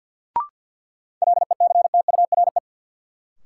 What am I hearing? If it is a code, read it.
Morse code, heard as a keyed tone, about 51 wpm, 700 Hz. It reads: LECTURE
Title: Sentences from Top 1600 Words - New Word Drill - 50wpm